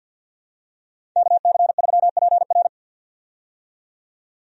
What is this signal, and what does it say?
Morse code, heard as a keyed tone, about 48 wpm, 700 Hz. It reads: KC3PR
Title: USA Call Signs - 5 Characters - 50wpm